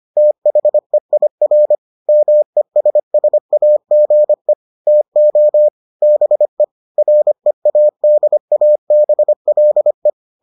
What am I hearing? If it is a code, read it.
Morse code, heard as a keyed tone, about 25 wpm, 600 Hz. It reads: THEIR MESSAGE TO BE READABLE